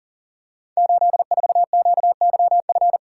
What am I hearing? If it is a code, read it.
Morse code, heard as a keyed tone, about 40 wpm, 690 Hz. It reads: 84QYF
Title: Code Groups - 5 Letter:Numbers - 40wpm